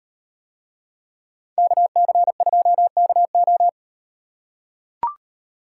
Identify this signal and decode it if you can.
Morse code, heard as a keyed tone, about 38 wpm, 700 Hz. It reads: KC2KO